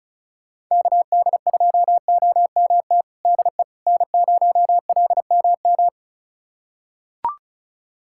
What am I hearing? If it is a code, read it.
Morse code, heard as a keyed tone, about 36 wpm, 700 Hz. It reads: KD2OMT DE N0LMM
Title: USA Callsign DE Callsign 35wpm